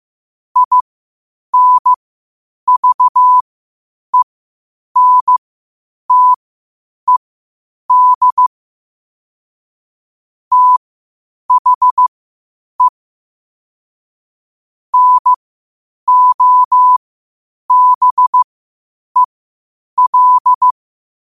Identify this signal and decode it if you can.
Morse code, heard as a keyed tone, about 15 wpm, spaced as for 8 wpm, 1000 Hz. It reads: INVENTED THE NOBEL
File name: qotdfiles / 8wpm